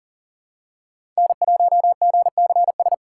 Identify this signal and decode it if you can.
Morse code, heard as a keyed tone, about 40 wpm, 700 Hz. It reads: N1GCS
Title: USA Call Signs - 5 Characters - 2x Character Spacing - 40wpm